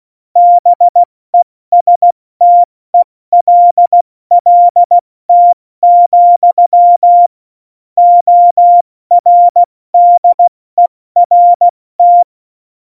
Morse code, heard as a keyed tone, about 16 wpm, 700 Hz. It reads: BESTELLT, ORDERT
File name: Morsekurs / Witze / Witz 10.Dezember 2019, 70 F80